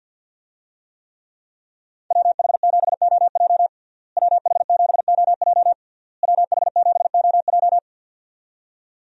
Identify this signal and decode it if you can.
Morse code, heard as a keyed tone, about 51 wpm, 700 Hz. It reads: WH7OJ WH7OJ WH7OJ
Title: USA Call Signs - 5 Characters - Repeated 3x - 50wpm